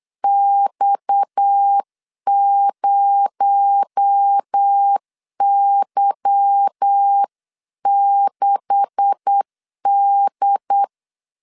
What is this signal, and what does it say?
Morse code, heard as a keyed tone, about 8 wpm, 800 Hz. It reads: X0Y6D
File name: CW / DET / 1 9 1500C S B 8wpm 5624